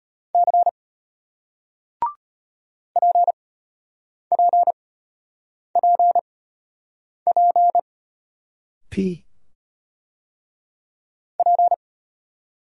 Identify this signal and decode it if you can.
Morse code, heard as a keyed tone, about 33 wpm, spaced as for 8 wpm, 700 Hz. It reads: C PPPP P